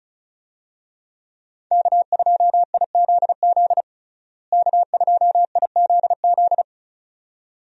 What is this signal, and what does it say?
Morse code, heard as a keyed tone, about 34 wpm, 700 Hz. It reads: K2IZZ K2IZZ
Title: USA Call Signs - 5 Characters - Repeated 2x - 35wpm